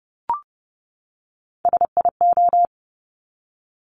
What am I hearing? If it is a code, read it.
Morse code, heard as a keyed tone, about 30 wpm, 690 Hz. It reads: SIO
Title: Top Three Letter Combinations in Words 30wpm